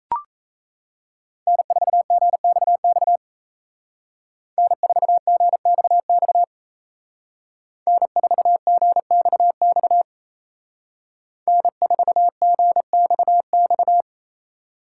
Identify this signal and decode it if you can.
Morse code, heard as a keyed tone, about 36 wpm, 700 Hz. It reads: N4GXX N4GXX N4GXX N4GXX